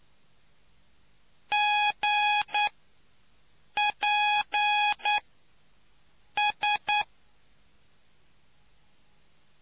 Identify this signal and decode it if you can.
Morse code, heard as a keyed tone, about 9 wpm, spaced as for 5 wpm, 830 Hz. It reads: GPS